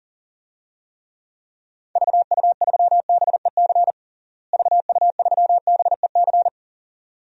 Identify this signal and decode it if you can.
Morse code, heard as a keyed tone, about 40 wpm, 700 Hz. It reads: VU3BEC VU3BEC
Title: DXCC-100 Call Signs - 6 Characters - Repeated 2x - 40wpm